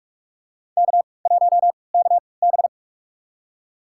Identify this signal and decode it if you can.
Morse code, heard as a keyed tone, about 46 wpm, spaced as for 25 wpm, 700 Hz. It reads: K1KB
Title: USA Call Signs - 4 Characters - 3x Character Spacing - 45wpm